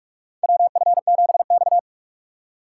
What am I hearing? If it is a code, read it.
Morse code, heard as a keyed tone, about 45 wpm, 700 Hz. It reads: WF7X
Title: USA Call Signs - 4 Characters - 3x Character Spacing - 45wpm